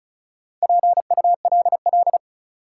Morse code, heard as a keyed tone, about 36 wpm, 700 Hz. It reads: PULL
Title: Sentences from Top 400 Words - New Word Drill - 35wpm